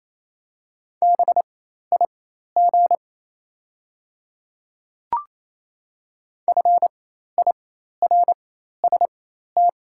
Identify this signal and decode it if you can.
Morse code, heard as a keyed tone, about 28 wpm, spaced as for 13 wpm, 700 Hz. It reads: BIG FIRST
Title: 4x Character Spacing - Top 200 Words - 28wpm